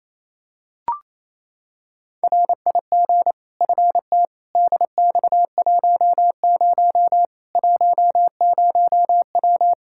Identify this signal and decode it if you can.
Morse code, heard as a keyed tone, about 28 wpm, 700 Hz. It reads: RIG FT DX10 10W
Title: Practice QSO - Exchange 2 - Info - 28wpm